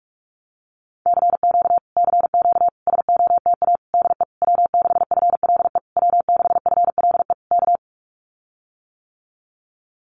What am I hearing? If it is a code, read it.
Morse code, heard as a keyed tone, about 46 wpm, 700 Hz. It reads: CQ CQ SOTA DE W6FLE W6FLE K
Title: Common QSO Phrases Encore 45wpm